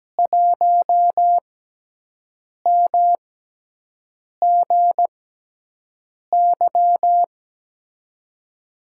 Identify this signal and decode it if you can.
Morse code, heard as a keyed tone, about 17 wpm, spaced as for 6 wpm, 700 Hz. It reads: 1MGY
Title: USA Call Signs - 5 Characters - 6x Character Spacing - 17wpm